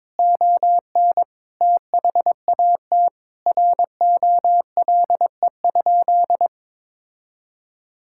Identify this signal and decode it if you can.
Morse code, heard as a keyed tone, about 22 wpm, 700 Hz. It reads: ON THAT ROLE?
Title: Sentences from Top 1000 Words - Review Part 6 - 22wpm